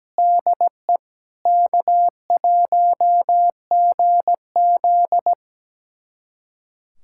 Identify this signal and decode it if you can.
Morse code, heard as a keyed tone, about 17 wpm, 700 Hz. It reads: DE K1GZ
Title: USA Callsign DE Callsign - 4 Characters - 17wpm